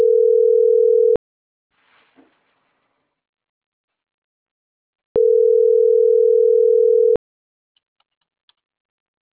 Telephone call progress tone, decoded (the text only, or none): ringback tone